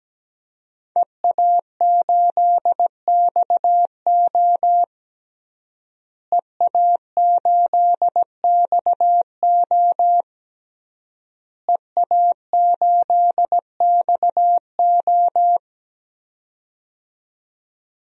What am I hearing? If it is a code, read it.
Morse code, heard as a keyed tone, about 17 wpm, 700 Hz. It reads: EA8XO EA8XO EA8XO
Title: DXCC-100 Call Signs - 5 Characters - Repeated 3x - 17wpm